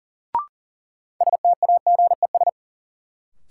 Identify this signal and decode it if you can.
Morse code, heard as a keyed tone, about 40 wpm, 700 Hz. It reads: STAGES
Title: Sets of 5 Words Encore - New Word Drill - 40wpm